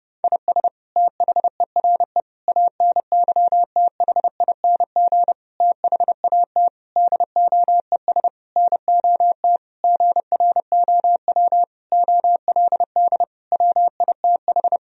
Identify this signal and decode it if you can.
Morse code, heard as a keyed tone, about 30 wpm, 700 Hz. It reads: IS THERE ANYTHING THAT DOES NOT GROW OLD WITH
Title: Sentences from Top 1000 Words - Review Part 1 - 30wpm